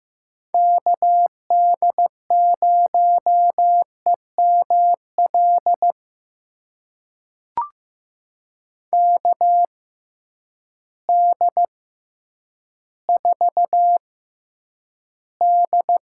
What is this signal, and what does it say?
Morse code, heard as a keyed tone, about 15 wpm, spaced as for 9 wpm, 700 Hz. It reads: KD0EML KD4D
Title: USA Call Signs - 6 Characters - 6x Character Spacing - 15wpm